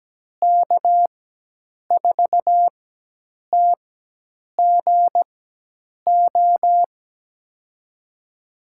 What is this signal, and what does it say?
Morse code, heard as a keyed tone, about 17 wpm, spaced as for 8 wpm, 700 Hz. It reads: K4TGO
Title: USA Call Signs - 4x Character Spacing - 17wpm